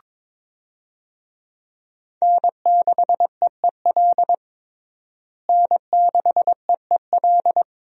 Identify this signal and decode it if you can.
Morse code, heard as a keyed tone, about 22 wpm, 700 Hz. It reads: N6EEL N6EEL
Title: USA Call Signs - 5 Characters - Repeated 2x - 22wpm